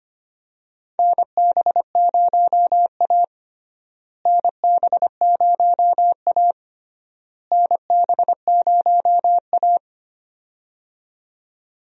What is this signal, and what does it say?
Morse code, heard as a keyed tone, about 25 wpm, 700 Hz. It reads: NB0A NB0A NB0A
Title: USA Call Signs - 4 Characters - Repeated 3x - 25wpm